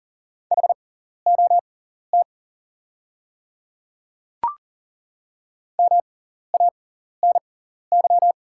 Morse code, heard as a keyed tone, about 40 wpm, spaced as for 14 wpm, 700 Hz. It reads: HOT MANY
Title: 6x Character Spacing - Top 300 Words - 40wpm